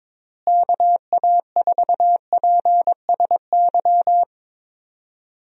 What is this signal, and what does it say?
Morse code, heard as a keyed tone, about 21 wpm, 700 Hz. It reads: KA4PSY
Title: USA Call Signs - 6 Characters - Repeated 2x - 22wpm